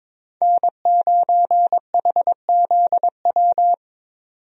Morse code, heard as a keyed tone, about 21 wpm, 700 Hz. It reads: N9HZW
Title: USA Call Signs - 5 Characters - 22wpm